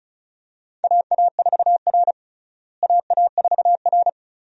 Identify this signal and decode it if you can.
Morse code, heard as a keyed tone, about 34 wpm, 700 Hz. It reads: AA4R AA4R
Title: USA Call Signs - 4 Characters - Repeated 2x - 35wpm